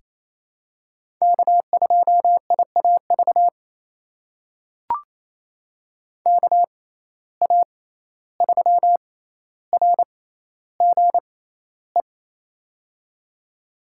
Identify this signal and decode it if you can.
Morse code, heard as a keyed tone, about 28 wpm, spaced as for 14 wpm, 700 Hz. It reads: K2IAV KA3RGE
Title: USA Call Signs - 6x Character Spacing - 28wpm